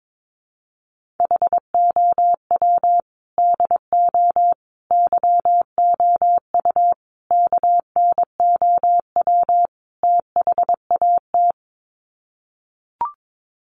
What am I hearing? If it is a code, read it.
Morse code, heard as a keyed tone, about 22 wpm, 700 Hz. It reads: HOW DO YOU KNOW THAT
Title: Sets of 5 Words Encore 22wpm